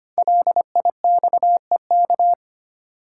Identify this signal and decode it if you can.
Morse code, heard as a keyed tone, about 25 wpm, 700 Hz. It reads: LIXEK
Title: Code Groups - 5 Letter:Numbers - 25wpm